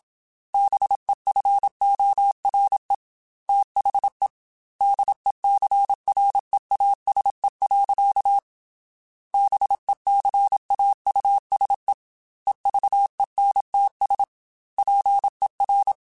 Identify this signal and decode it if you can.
Morse code, heard as a keyed tone, about 27 wpm, 800 Hz. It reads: BEFORE THE DECREASE. BECAUSE EVENTS PER
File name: Technical Subjects / Technical Subjects-mp3s / XC @25